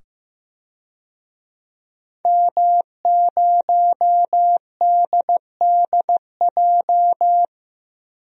Morse code, heard as a keyed tone, about 15 wpm, 700 Hz. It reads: M0DDJ
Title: DXCC-100 Call Signs - 5 Characters - 15wpm